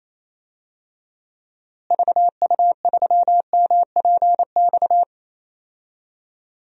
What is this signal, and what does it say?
Morse code, heard as a keyed tone, about 28 wpm, 700 Hz. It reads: VU3MPX
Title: DXCC-100 Call Signs - 6 Characters - Repeated 2x - 28wpm